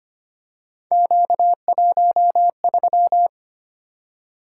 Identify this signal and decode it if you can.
Morse code, heard as a keyed tone, about 25 wpm, 700 Hz. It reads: Q13
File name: Three Letter-Number Combinations - 25wpm